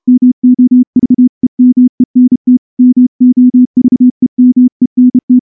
Morse code, heard as a keyed tone, about 28 wpm, 250 Hz. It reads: MOVEMENT MOVEMENT
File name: morse-training-30WPM